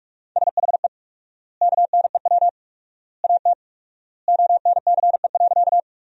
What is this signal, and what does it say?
Morse code, heard as a keyed tone, about 46 wpm, 700 Hz. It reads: SHE KNEW AT ONCE.
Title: 4x Word-Spacing - Sentences from Top 400 Words - 45wpm